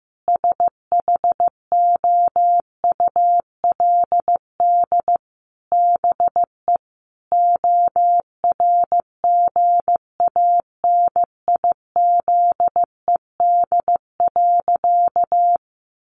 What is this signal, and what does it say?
Morse code, heard as a keyed tone, about 15 wpm, 700 Hz. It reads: SHOULD BE ORGANIZED.
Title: Sentences from Top 1300 Words 15wpm